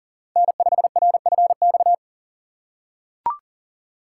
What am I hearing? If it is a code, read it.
Morse code, heard as a keyed tone, about 37 wpm, 700 Hz. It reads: N5RFX
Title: USA Call Signs - 4x Character Spacing - 40wpm